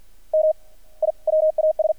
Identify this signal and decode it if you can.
Morse code, heard as a keyed tone, about 50 wpm, 620 Hz. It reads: M I WAS
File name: archived / CRSnet